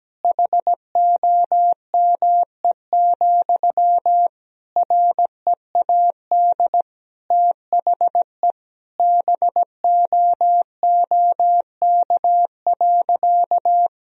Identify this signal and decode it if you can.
Morse code, heard as a keyed tone, about 17 wpm, 700 Hz. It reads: HOME, READ THE BOOK.
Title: Sentences from Top 1000 Words - Review Part 2 - 17wpm